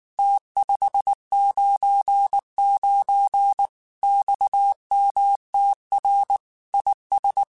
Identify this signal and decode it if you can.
Morse code, heard as a keyed tone, about 19 wpm, 800 Hz. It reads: T599 XMTR IS
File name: QSOs / QSOs-mp3s / WA7QOX 1@18